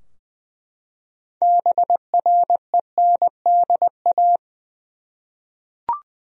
Morse code, heard as a keyed tone, about 20 wpm, 700 Hz. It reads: BRENDA
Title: USA Common Names - 20wpm